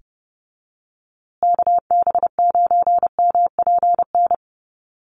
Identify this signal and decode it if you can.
Morse code, heard as a keyed tone, about 30 wpm, 700 Hz. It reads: KB9MPN
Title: USA Call Signs Repeated 3x 30wpm